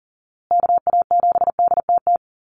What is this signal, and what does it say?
Morse code, heard as a keyed tone, about 40 wpm, 700 Hz. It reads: KA7DTT
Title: USA Call Signs Repeated 2x 40wpm